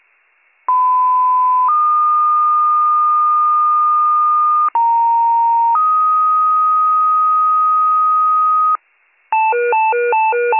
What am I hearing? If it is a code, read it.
Two-tone page, heard as a one-second tone, then a three-second tone, about 1010 then 1220 Hz, about 930 then 1250 Hz. It is heard from 0.7 to 8.8 s.